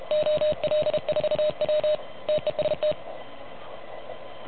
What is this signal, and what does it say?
Morse code, heard as a keyed tone, about 33 wpm, 610 Hz. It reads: OL4W TEST
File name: truSDX / MWC29.8